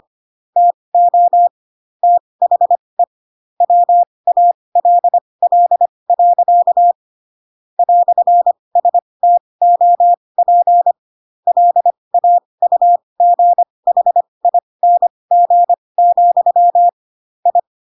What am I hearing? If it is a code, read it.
Morse code, heard as a keyed tone, about 25 wpm, spaced as for 20 wpm, 700 Hz. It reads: TO THE WALL. "STOP LAUGHING, I
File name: Pinocchio / Pinocchio ch3 15 25 ik1hge